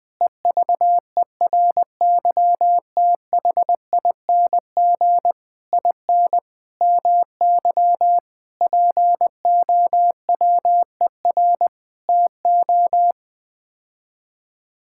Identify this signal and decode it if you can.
Morse code, heard as a keyed tone, about 21 wpm, 700 Hz. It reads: EVERYTHING IN MY POWER TO
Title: Sets of 8 Words - 20wpm